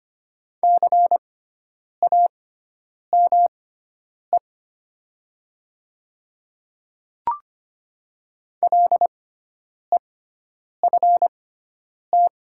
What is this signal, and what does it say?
Morse code, heard as a keyed tone, about 25 wpm, spaced as for 9 wpm, 700 Hz. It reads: CAME LEFT
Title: 6x Character Spacing - Top 500 Words - 25wpm